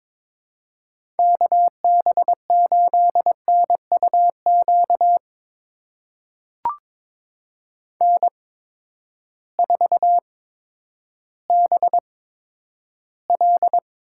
Morse code, heard as a keyed tone, about 22 wpm, spaced as for 11 wpm, 700 Hz. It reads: KB8NUQ N4BL
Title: USA Call Signs - 8x Character Spacing - 22wpm